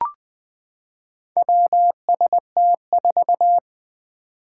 Morse code, heard as a keyed tone, about 20 wpm, 700 Hz. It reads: WST4
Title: Four Challenging Letter-Number Combinations 20wpm